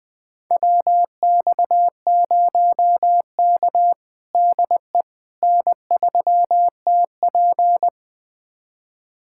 Morse code, heard as a keyed tone, about 20 wpm, 700 Hz. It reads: WX0K DE N3TP
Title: USA Callsign DE Callsign - 4 Characters - 20wpm